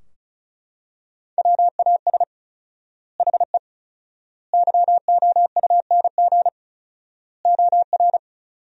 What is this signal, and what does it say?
Morse code, heard as a keyed tone, about 36 wpm, 700 Hz. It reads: WAS HE YOUNG OR
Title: 4x Word-Spacing - Sentences from Top 400 Words - 35wpm